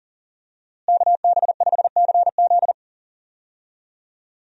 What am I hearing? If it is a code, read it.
Morse code, heard as a keyed tone, about 40 wpm, 700 Hz. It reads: KB5CZ
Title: USA Call Signs - 5 Characters - Repeated 2x - 40wpm